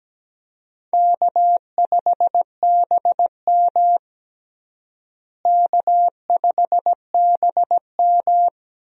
Morse code, heard as a keyed tone, about 17 wpm, 700 Hz. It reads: K5BM K5BM
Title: USA Call Signs - 4 Characters - Repeated 3x - 17wpm